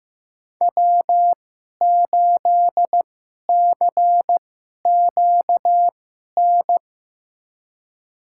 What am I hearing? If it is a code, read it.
Morse code, heard as a keyed tone, about 15 wpm, spaced as for 11 wpm, 700 Hz. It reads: W8CQN
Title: USA Call Signs - 5 Characters - 2x Character Spacing - 15wpm